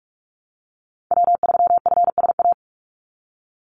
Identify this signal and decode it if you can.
Morse code, heard as a keyed tone, about 44 wpm, 700 Hz. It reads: W3FSA
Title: Call Signs Repeated 3x - 45wpm